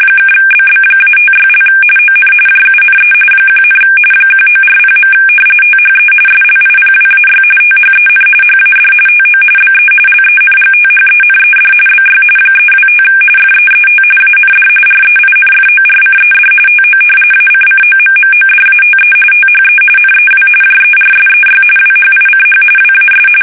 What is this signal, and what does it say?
RTTY, about 45.45 baud. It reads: (BB-56) ON JUNE 1, 1940. THE SLIPWAY THEN HAD TO BE LENGTHENED BY 325 FEET AND REINFORCED TO ACCOMMODATE THE NEW JERSEY WHICH WAS
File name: fsk850 / battleshipnj